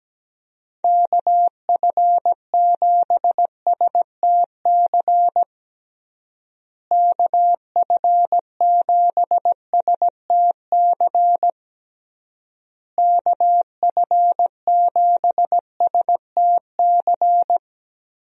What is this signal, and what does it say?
Morse code, heard as a keyed tone, about 17 wpm, 700 Hz. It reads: KF7STC KF7STC KF7STC
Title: USA Call Signs - 6 Characters - Repeated 3x - 17wpm